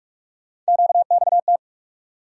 Morse code, heard as a keyed tone, about 43 wpm, 700 Hz. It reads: QXT
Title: USA Call Sign Postfixes - 3 Letters 45wpm